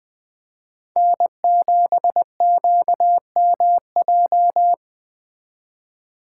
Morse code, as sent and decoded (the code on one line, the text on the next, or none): -. --... --.- -- .---
N7QMJ